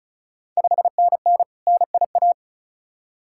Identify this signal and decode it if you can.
Morse code, heard as a keyed tone, about 36 wpm, 700 Hz. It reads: 5NN NIA
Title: Ontario QSO Party Exchange 35wpm